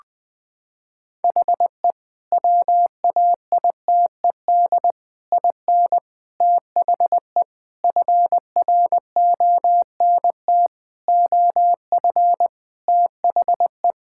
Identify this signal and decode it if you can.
Morse code, heard as a keyed tone, about 21 wpm, 700 Hz. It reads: HE WAITED IN THE FRONT OF THE
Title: Sentences with 4-Word Phrases - 20wpm